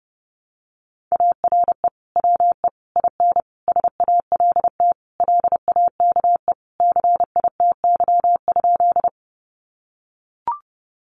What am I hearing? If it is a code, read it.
Morse code, heard as a keyed tone, about 30 wpm, 700 Hz. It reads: ARE WE IN SALT LAKE CITY?
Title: Sentences from Top 1100 Words 30wpm